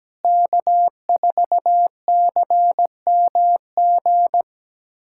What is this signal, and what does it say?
Morse code, heard as a keyed tone, about 17 wpm, 700 Hz. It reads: K4CMG
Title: USA Callsign DE Callsign - 5 Characters - 17wpm